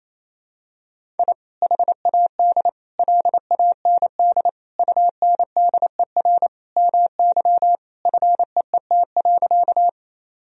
Morse code, heard as a keyed tone, about 28 wpm, 700 Hz. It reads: I HAD LAND UNDER MY FEET.
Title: Sentences from Top 1000 Words - Review Part 1 - 28wpm